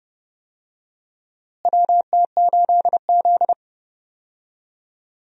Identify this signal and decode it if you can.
Morse code, heard as a keyed tone, about 30 wpm, 700 Hz. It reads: WT8Z
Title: USA Call Signs - 4 Characters - Repeated 3x - 30wpm